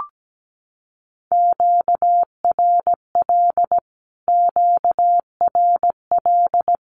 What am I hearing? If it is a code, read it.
Morse code, heard as a keyed tone, about 17 wpm, 700 Hz. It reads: QRL QRL
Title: Q-Codes Repeated 3x 17wpm